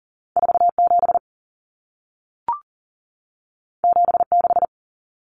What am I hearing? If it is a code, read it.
Morse code, heard as a keyed tone, about 40 wpm, 700 Hz. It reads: 47 76
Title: Numbers - 2-Digit 40wpm